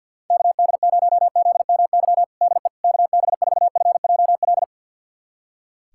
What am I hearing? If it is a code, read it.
Morse code, heard as a keyed tone, about 50 wpm, 700 Hz. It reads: KD0ZMY DE KB4FJL
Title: USA Callsign DE Callsign - 6 Characters - 50wpm